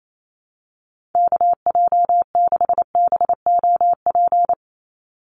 Morse code, heard as a keyed tone, about 28 wpm, 700 Hz. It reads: KJ6BOP
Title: USA Call Signs 28wpm